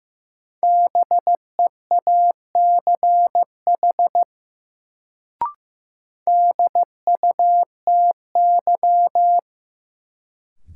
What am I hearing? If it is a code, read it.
Morse code, heard as a keyed tone, about 15 wpm, 700 Hz. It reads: BEACH DUTY
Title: Sentences from Top 1500 Words - New Word Drill - 15wpm